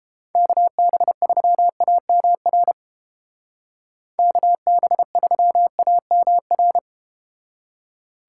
Morse code, heard as a keyed tone, about 32 wpm, 700 Hz. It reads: KB3AMR KB3AMR